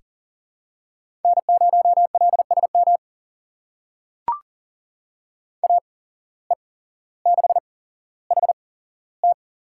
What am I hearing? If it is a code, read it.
Morse code, heard as a keyed tone, about 40 wpm, spaced as for 17 wpm, 700 Hz. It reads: N0LSM AE6HT
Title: USA Call Signs - 5 Characters - 8x Character Spacing - 40wpm